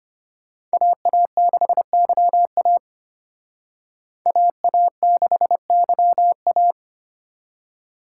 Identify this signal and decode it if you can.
Morse code, heard as a keyed tone, about 31 wpm, 700 Hz. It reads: AA6YA AA6YA